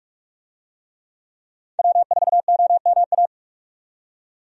Morse code, heard as a keyed tone, about 45 wpm, 700 Hz. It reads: W4OMA
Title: USA Call Signs - 5 Characters - 45wpm